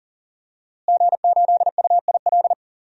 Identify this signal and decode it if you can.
Morse code, heard as a keyed tone, about 40 wpm, 700 Hz. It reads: G8UIL
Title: DXCC-100 Call Signs - 5 Characters - Repeated 3x - 40wpm